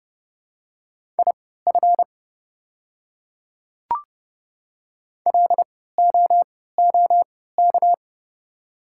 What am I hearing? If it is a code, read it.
Morse code, heard as a keyed tone, about 31 wpm, spaced as for 17 wpm, 700 Hz. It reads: IF LOOK
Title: 3x Character Spacing - Top 100 Words - 30wpm